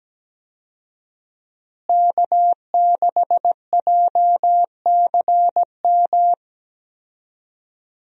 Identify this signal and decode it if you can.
Morse code, heard as a keyed tone, about 17 wpm, 700 Hz. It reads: K6JCM
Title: USA Call Signs - 5 Characters - Repeated 2x - 17wpm